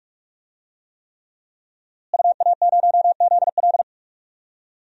Morse code, heard as a keyed tone, about 46 wpm, 700 Hz. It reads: UA0ZL
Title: DXCC-100 Call Signs - 5 Characters - Repeated 3x - 45wpm